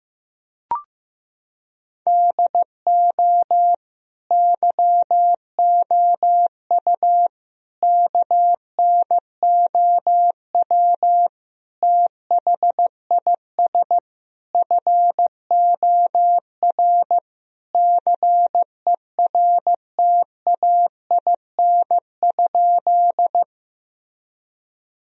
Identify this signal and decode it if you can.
Morse code, heard as a keyed tone, about 16 wpm, 700 Hz. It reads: DO YOU KNOW THIS FOR CERTAIN?
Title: Sentences from Top 1000 Words - Review Part 2 - 15wpm